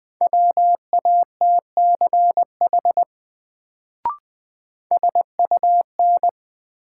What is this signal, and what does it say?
Morse code, heard as a keyed tone, about 20 wpm, 700 Hz. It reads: WATCH SUN
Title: Sentences from Top 300 Words - New Word Drill - 20wpm